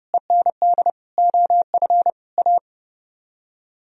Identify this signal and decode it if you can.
Morse code, heard as a keyed tone, about 35 wpm, 700 Hz. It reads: END OF A
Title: Speed-Racing - Sets of 5 Words 30wpm